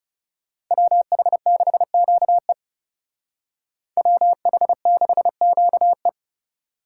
Morse code, heard as a keyed tone, about 37 wpm, 700 Hz. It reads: WH6QE WH6QE
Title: Speed Racing - USA 2x2 Call Signs 30wpm